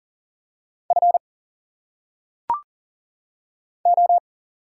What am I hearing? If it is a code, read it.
Morse code, heard as a keyed tone, about 40 wpm, 700 Hz. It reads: F O
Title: Single Letters - Phonetic Pronunciation 40wpm